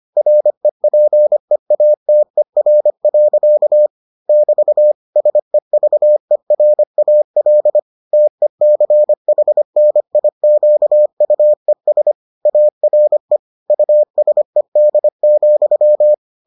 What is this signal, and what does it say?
Morse code, heard as a keyed tone, about 25 wpm, 600 Hz. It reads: REPEATER. <BT> SEVERAL TECHNIQUES ARE USED,